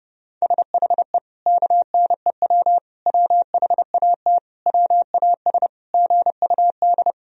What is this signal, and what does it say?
Morse code, heard as a keyed tone, about 30 wpm, 700 Hz. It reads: SHE KNEW WHAT WAS GUD
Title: Sentences from Top 400 Words and Abbreviations 30wpm